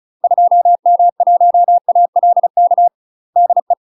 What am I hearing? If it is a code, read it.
Morse code, heard as a keyed tone, about 36 wpm, 700 Hz. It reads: 2M1ALK DE
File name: UK MASTER QSOs TRAINING / 35WPM / UK MASTER QSOs 0009